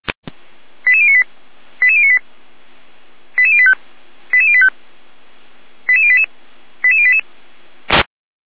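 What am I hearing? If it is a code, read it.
Five-tone selective call (ZVEI), heard as quick runs of five tones, tones of about 70 ms. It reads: 88098 88098 88085 88085 88088 88088
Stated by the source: ZVEI five-tone call received on SDR.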